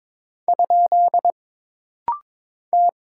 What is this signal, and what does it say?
Morse code, heard as a keyed tone, about 22 wpm, 700 Hz. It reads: ? T
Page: This rapid-fire timing is geared to push your ICR (Instant Character Recognition) skills.